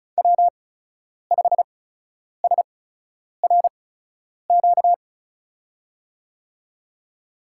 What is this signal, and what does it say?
Morse code, heard as a keyed tone, about 36 wpm, spaced as for 10 wpm, 700 Hz. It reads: W5SRQ
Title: USA Call Signs - 5 Characters - 8x Character Spacing - 35wpm